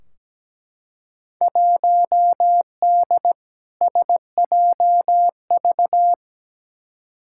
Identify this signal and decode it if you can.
Morse code, heard as a keyed tone, about 17 wpm, 700 Hz. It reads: 1D SJV
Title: Field Day Exchanges Encore - 17wpm